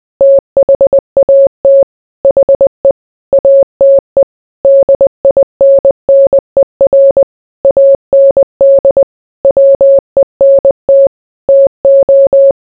Morse code, heard as a keyed tone, about 20 wpm, 550 Hz. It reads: THAT HE ATE DINNER AND WENT TO